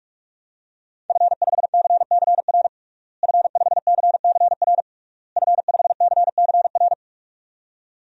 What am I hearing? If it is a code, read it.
Morse code, heard as a keyed tone, about 46 wpm, 700 Hz. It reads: F5CCR F5CCR F5CCR
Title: DXCC-100 Call Signs - 5 Characters - Repeated 3x - 45wpm